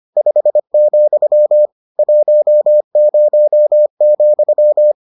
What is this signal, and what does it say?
Morse code, heard as a keyed tone, about 25 wpm, 600 Hz. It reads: 5, 10,